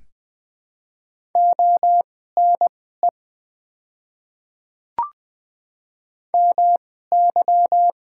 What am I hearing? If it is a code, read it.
Morse code, heard as a keyed tone, about 20 wpm, spaced as for 14 wpm, 700 Hz. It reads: ONE MY
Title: 2x Character Spacing - Top 200 Words - 20wpm